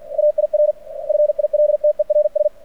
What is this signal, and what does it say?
Morse code, heard as a keyed tone, about 50 wpm, 600 Hz. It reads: AIR FILTERS
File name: archived / CRSnet